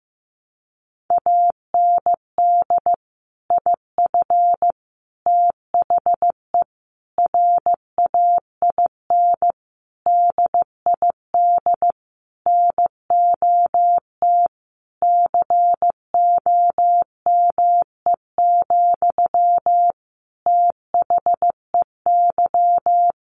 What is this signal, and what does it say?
Morse code, heard as a keyed tone, about 15 wpm, 700 Hz. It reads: AND IF THE RAIN DID NOT COME, THEY
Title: Sentences from Top 1300 Words 15wpm